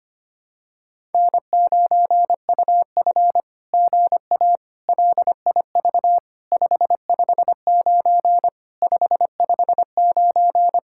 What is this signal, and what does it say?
Morse code, heard as a keyed tone, about 24 wpm, 700 Hz. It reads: N9UF GA LIV 559 559
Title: SOTA QSO Practice 25wpm